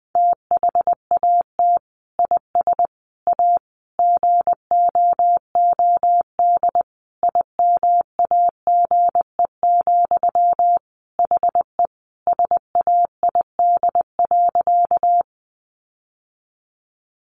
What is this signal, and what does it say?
Morse code, heard as a keyed tone, about 20 wpm, 700 Hz. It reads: THAT IS A GOOD IMAGE, HE SAID.
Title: Sentences from Top 700 Words 20wpm